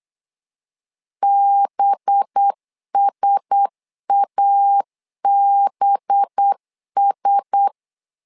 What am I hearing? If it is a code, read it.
Morse code, heard as a keyed tone, about 8 wpm, 800 Hz. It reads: BSABS
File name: LEZIONI / 5 LBVC / 5 LB 8wpm 2816